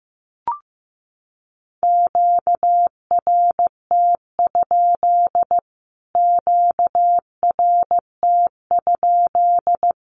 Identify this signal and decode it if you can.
Morse code, heard as a keyed tone, about 15 wpm, 700 Hz. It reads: QRT? QRT?
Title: Q-Codes Repeated 3x 15wpm